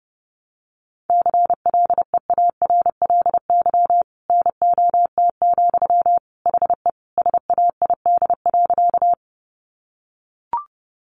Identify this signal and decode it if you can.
Morse code, heard as a keyed tone, about 28 wpm, 700 Hz. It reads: CLEARLY NOT, HE SAID.
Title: Sentences from Top 1300 Words 30wpm